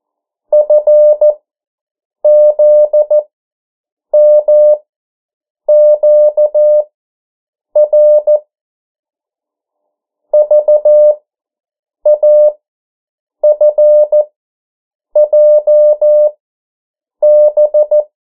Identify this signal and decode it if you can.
Morse code, heard as a keyed tone, about 14 wpm, spaced as for 7 wpm, 600 Hz. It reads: FZMQR VAFJB
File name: morse mp3 / 09 urok 8